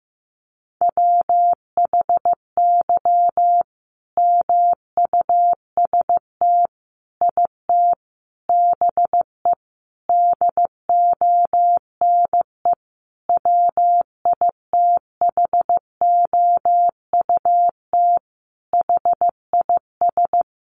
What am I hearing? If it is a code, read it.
Morse code, heard as a keyed tone, about 14 wpm, 700 Hz. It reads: WHY MUST IT BE DONE WITHOUT HIS
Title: Sentences from Top 600 Words 15wpm